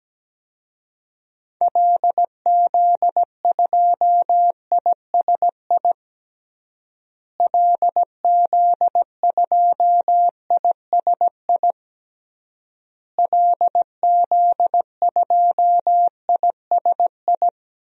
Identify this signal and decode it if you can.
Morse code, heard as a keyed tone, about 17 wpm, 700 Hz. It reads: LZ2ISI LZ2ISI LZ2ISI
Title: DXCC-100 Call Signs - 6 Characters - Repeated 3x - 17wpm